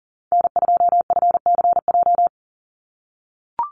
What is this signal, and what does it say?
Morse code, heard as a keyed tone, about 38 wpm, 700 Hz. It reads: N2FCJ
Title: USA Call Signs 40wpm